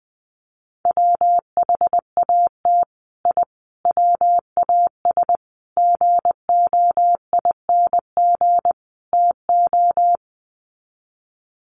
Morse code, heard as a keyed tone, about 20 wpm, 700 Hz. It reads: WHAT I WAS GOING TO
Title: Sets of 5 Words Encore2 20wpm